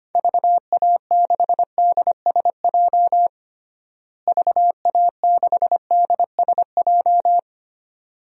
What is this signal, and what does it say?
Morse code, heard as a keyed tone, about 25 wpm, 700 Hz. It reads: VA6DSJ VA6DSJ
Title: DXCC-100 Call Signs - 6 Characters - Repeated 2x - 25wpm